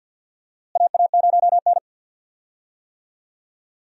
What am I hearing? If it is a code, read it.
Morse code, heard as a keyed tone, about 52 wpm, 700 Hz. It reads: AA0N
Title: USA Call Signs - 4 Characters - Repeated 2x - 50wpm